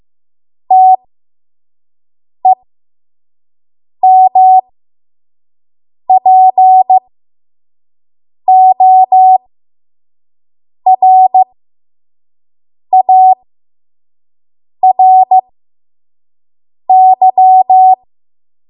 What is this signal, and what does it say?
Morse code, heard as a keyed tone, about 15 wpm, spaced as for 5 wpm, 750 Hz. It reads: TEMPORARY
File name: Archive / 5 WPM / 05WPM